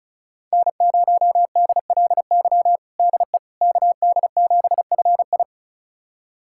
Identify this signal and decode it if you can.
Morse code, heard as a keyed tone, about 36 wpm, 700 Hz. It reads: N0DLY DE KD7FI
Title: USA Callsign DE Callsign - 5 Characters - 35wpm